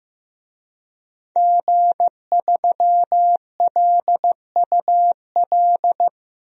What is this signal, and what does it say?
Morse code, heard as a keyed tone, about 15 wpm, 700 Hz. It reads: G3LUL
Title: DXCC-100 Call Signs - 5 Characters - Repeated 3x - 15wpm